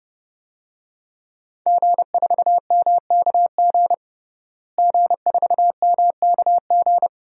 Morse code, heard as a keyed tone, about 32 wpm, 700 Hz. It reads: G4MKG G4MKG
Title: DXCC-100 Call Signs - 5 Characters - Repeated 3x - 30wpm